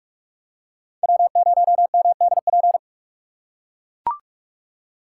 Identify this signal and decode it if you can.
Morse code, heard as a keyed tone, about 44 wpm, 700 Hz. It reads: W0MDP
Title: USA Call Signs - 5 Characters - 6x Character Spacing - 45wpm